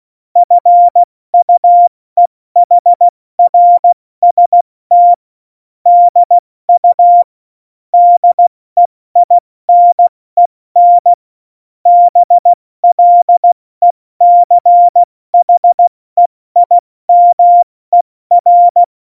Morse code, heard as a keyed tone, about 16 wpm, spaced as for 14 wpm, 700 Hz. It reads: FUEHRST DU DEINEN BLECHEIMER
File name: Morsekurs / Witze / Witz 04 August 2020, 70 F80